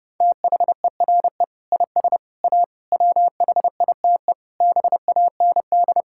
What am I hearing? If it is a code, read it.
Morse code, heard as a keyed tone, about 30 wpm, 700 Hz. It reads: THERE IS A WHITE BAND
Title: Sentences from Top 1600 Words 30wpm